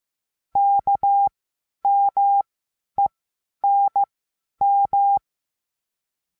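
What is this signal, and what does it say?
Morse code, heard as a keyed tone, about 15 wpm, spaced as for 10 wpm, 800 Hz. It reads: KMENM
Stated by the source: Morse lesson adding the letter N, sent as five-letter code groups.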